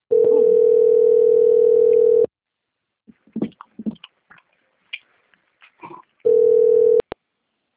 Ringback tone (North America), 440 + 480 Hz.